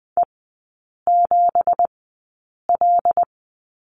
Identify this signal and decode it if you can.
Morse code, heard as a keyed tone, about 20 wpm, spaced as for 8 wpm, 690 Hz. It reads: E7L
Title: 166 Lesson 27e - Number 7 - Three Characters 20wpm